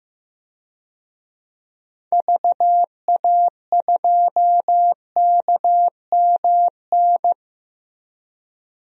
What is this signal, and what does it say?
Morse code, heard as a keyed tone, about 15 wpm, 700 Hz. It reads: VA2KMN
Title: DXCC-100 Call Signs - 6 Characters - Repeated 3x - 15wpm